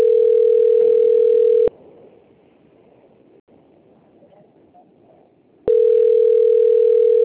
Ringback tone (North America), 440 + 480 Hz.
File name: out- - - - - .3090